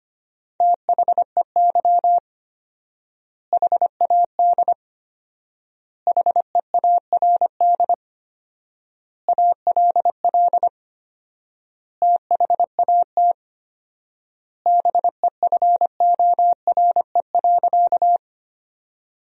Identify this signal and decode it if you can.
Morse code, heard as a keyed tone, about 25 wpm, 700 Hz. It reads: THEY HAD HEARD ALL THAT BEFORE.
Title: 4x Word-Spacing - Sentences from Top 400 Words - 25wpm